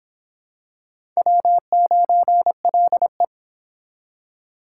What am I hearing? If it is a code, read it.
Morse code, heard as a keyed tone, about 26 wpm, 700 Hz. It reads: W9LE